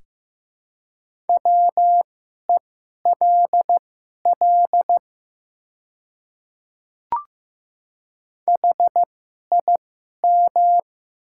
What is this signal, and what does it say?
Morse code, heard as a keyed tone, about 15 wpm, spaced as for 11 wpm, 690 Hz. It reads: WELL HIM
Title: 2x Character Spacing - Top 200 Words - 15wpm